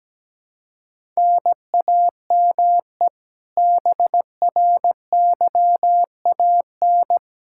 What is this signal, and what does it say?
Morse code, heard as a keyed tone, about 17 wpm, 700 Hz. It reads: NAME BRYAN
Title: K1USN SST QSO Practice 17wpm